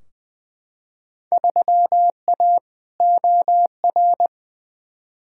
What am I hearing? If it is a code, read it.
Morse code, heard as a keyed tone, about 21 wpm, 700 Hz. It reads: 3A OR
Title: Field Day Exchanges Encore - 20wpm